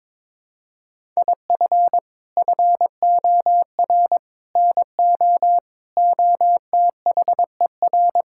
Morse code, heard as a keyed tone, about 22 wpm, 700 Hz. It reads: IF FOR NO OTHER
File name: practice / 22wpm / Sets of 6 Words - 22wpm